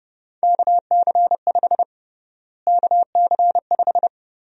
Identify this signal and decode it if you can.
Morse code, heard as a keyed tone, about 30 wpm, 700 Hz. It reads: KC5 KC5
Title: USA Call Sign Prefixes - Repeated 2x - 30wpm